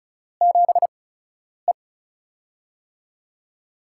Morse code, heard as a keyed tone, about 36 wpm, spaced as for 10 wpm, 700 Hz. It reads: 7E
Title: USA Call Signs - 8x Character Spacing - 35wpm